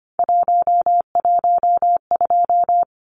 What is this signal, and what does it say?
Morse code, heard as a keyed tone, about 25 wpm, 700 Hz. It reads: 112
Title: Numbers - 3-Digit 25wpm